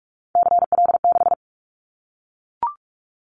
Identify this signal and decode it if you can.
Morse code, heard as a keyed tone, about 44 wpm, 700 Hz. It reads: CL6
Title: Callsign Prefixes for Top 100 DXCC - 45wpm